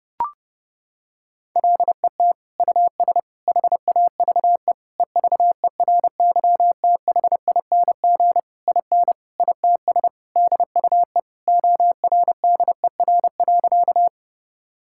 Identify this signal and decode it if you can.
Morse code, heard as a keyed tone, about 30 wpm, 700 Hz. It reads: LET US HAVE EVERYTHING IN ITS DUE ORDER.
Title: Sentences from Top 1000 Words - Review Part 6 - 30wpm